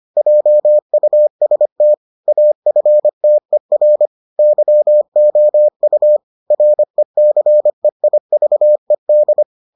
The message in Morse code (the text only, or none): JUST AFTER YOU RECEIVED